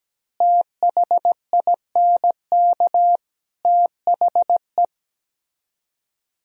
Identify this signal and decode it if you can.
Morse code, heard as a keyed tone, about 17 wpm, 700 Hz. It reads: THINK THE
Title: Speed-Racing - Sets of 3 Words 17wpm